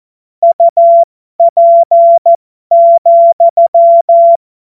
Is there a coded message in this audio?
Yes: Morse code from 0.4 to 4.4 s, heard as a keyed tone, about 14 wpm, spaced as for 12 wpm, 680 Hz. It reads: UP,